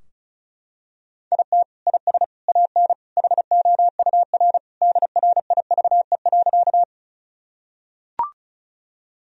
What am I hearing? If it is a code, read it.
Morse code, heard as a keyed tone, about 36 wpm, 700 Hz. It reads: IT IS AN HOUR DRIVE.
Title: Sentences from Top 1000 Words - Review Part 2 - 35wpm